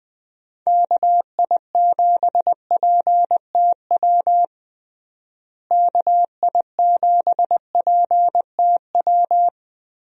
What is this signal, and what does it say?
Morse code, heard as a keyed tone, about 20 wpm, 700 Hz. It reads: KI7PTW KI7PTW
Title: USA Call Signs - 6 Characters - Repeated 2x - 20wpm